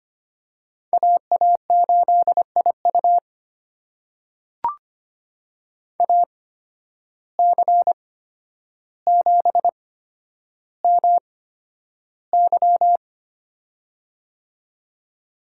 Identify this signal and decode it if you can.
Morse code, heard as a keyed tone, about 25 wpm, spaced as for 11 wpm, 700 Hz. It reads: AA8IU AC7MY
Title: USA Call Signs - 5 Characters - 8x Character Spacing - 25wpm